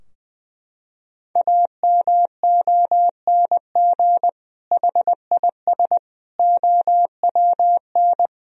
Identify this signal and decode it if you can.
Morse code, heard as a keyed tone, about 20 wpm, 700 Hz. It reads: AMONG HIS OWN
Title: Sentences from Top 1000 Words - Review Part 3 - 20wpm